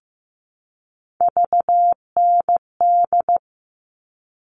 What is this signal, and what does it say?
Morse code, heard as a keyed tone, about 15 wpm, 700 Hz. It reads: VND
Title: All Cut Numbers - 3-Digit 15wpm